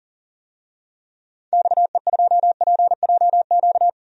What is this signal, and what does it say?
Morse code, heard as a keyed tone, about 39 wpm, 700 Hz. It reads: XE2PJQ
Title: DXCC-100 Call Signs - 6 Characters - 40wpm